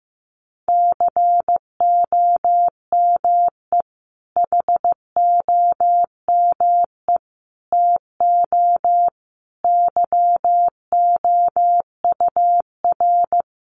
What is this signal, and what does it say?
Morse code, heard as a keyed tone, about 15 wpm, 700 Hz. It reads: COME HOME TO YOUR
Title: Sentences from Top 1000 Words 15wpm